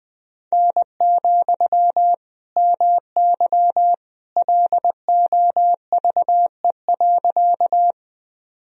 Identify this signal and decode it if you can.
Morse code, heard as a keyed tone, about 21 wpm, 700 Hz. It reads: N, MY LOVE.
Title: Sentences from Top 400 Words and Abbreviations 20wpm